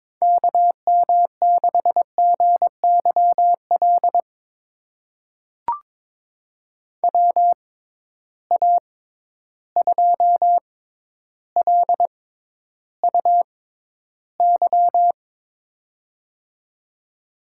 Morse code, heard as a keyed tone, about 22 wpm, spaced as for 11 wpm, 700 Hz. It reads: KM6GYL WA2LUY
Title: USA Call Signs - 6 Characters - 6x Character Spacing - 22wpm